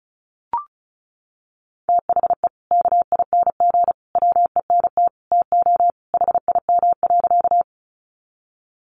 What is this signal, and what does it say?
Morse code, heard as a keyed tone, about 34 wpm, 700 Hz. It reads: THE KING WENT TO HIM.
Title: Sentences from Top 400 Words 35wpm